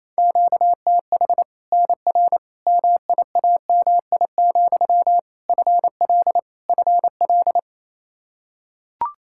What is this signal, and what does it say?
Morse code, heard as a keyed tone, about 28 wpm, 700 Hz. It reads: QTH NR MIAMI, FL FL
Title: Practice QSO - Exchange 1 - Beginning - 28wpm